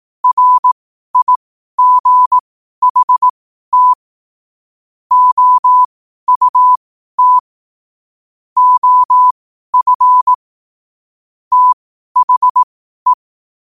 Morse code, heard as a keyed tone, about 18 wpm, spaced as for 12 wpm, 1000 Hz. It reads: RIGHT OUT OF THE
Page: Quotes for Sun, 27 Apr 2025 in Morse Code at 12 words per minute.